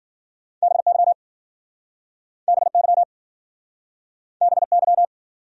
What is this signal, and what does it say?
Morse code, heard as a keyed tone, about 52 wpm, 700 Hz. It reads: BY BY BY